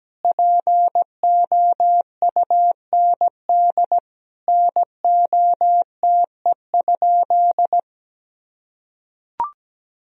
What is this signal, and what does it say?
Morse code, heard as a keyed tone, about 17 wpm, 700 Hz. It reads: POUND NOTE?
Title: Sentences from Top 1000 Words - Review Part 2 - 17wpm